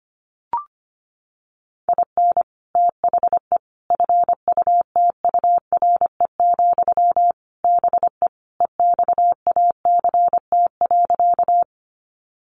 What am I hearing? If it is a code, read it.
Morse code, heard as a keyed tone, about 25 wpm, 700 Hz. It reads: IN THE FUTURE, BE EXACT.
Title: Sentences from Top 1300 Words 25wpm